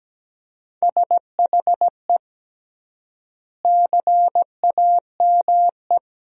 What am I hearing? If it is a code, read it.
Morse code, heard as a keyed tone, about 17 wpm, 700 Hz. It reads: SHE CAME
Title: 3x Word-Spacing - Sentences from Top 300 Words - 17wpm